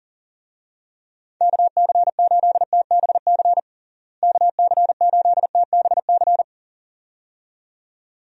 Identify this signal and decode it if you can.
Morse code, heard as a keyed tone, about 37 wpm, 700 Hz. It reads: KC8TBC KC8TBC
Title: USA Call Signs - 6 Characters - Repeated 2x - 40wpm